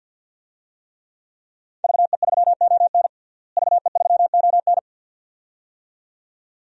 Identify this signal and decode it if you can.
Morse code, heard as a keyed tone, about 52 wpm, 700 Hz. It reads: VE3ON VE3ON
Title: DXCC-100 Call Signs - 5 Characters - Repeated 2x - 50wpm